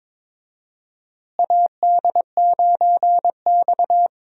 Morse code, heard as a keyed tone, about 22 wpm, 700 Hz. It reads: AD9X
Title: USA Call Signs - 4 Characters - Repeated 2x - 22wpm